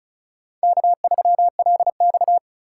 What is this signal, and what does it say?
Morse code, heard as a keyed tone, about 34 wpm, 700 Hz. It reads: K3LX
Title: USA Call Signs - 4 Characters - 6x Character Spacing - 35wpm